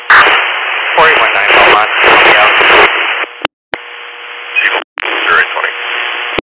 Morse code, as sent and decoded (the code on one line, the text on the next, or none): none